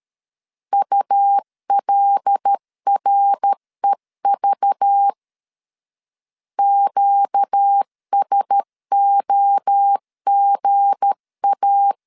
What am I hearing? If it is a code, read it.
Morse code, heard as a keyed tone, about 13 wpm, 800 Hz. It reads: ULREV QSOGA
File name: LEZIONI / 7 GF9Z / 7 G 12wpm 2055